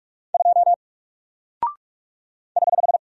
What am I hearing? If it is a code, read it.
Morse code, heard as a keyed tone, about 44 wpm, 700 Hz. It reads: 2 <HH>
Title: Single Character (Letter:Number:Punctuation:Prosign) 45wpm